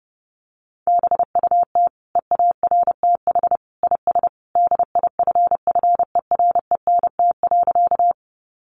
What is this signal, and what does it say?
Morse code, heard as a keyed tone, about 30 wpm, 700 Hz. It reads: BUT EARTH IS DIFFERENT.
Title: Sentences from Top 600 Words 30wpm